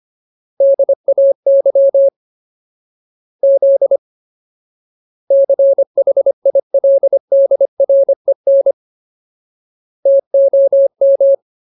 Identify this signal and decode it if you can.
Morse code, heard as a keyed tone, about 25 wpm, 550 Hz. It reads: DAY Z CHILDREN TOM
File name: Ultimate Word List / 25WPM Ultimate